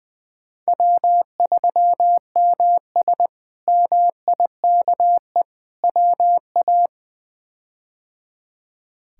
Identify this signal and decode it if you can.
Morse code, heard as a keyed tone, about 20 wpm, 700 Hz. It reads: W3MS MIKE WA
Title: K1USN SST QSO Practice - 20wpm